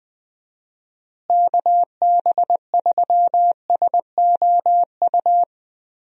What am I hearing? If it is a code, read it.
Morse code, heard as a keyed tone, about 20 wpm, 700 Hz. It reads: KB3SOU
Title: USA Call Signs - 6 Characters - Repeated 3x - 20wpm